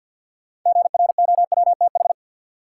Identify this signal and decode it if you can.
Morse code, heard as a keyed tone, about 46 wpm, 700 Hz. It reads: GROWTH
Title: Sentences from Top 1000 Words - Review Part 5 - New Word Drill - 50wpm